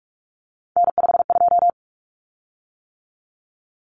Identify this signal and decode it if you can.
Morse code, heard as a keyed tone, about 44 wpm, 700 Hz. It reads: N52
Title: Common Cut Numbers - 3-Digit 45wpm